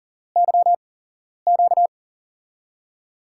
Morse code, heard as a keyed tone, about 40 wpm, spaced as for 11 wpm, 700 Hz. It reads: YQ
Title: USA Call Signs - 6 Characters - 8x Character Spacing - 40wpm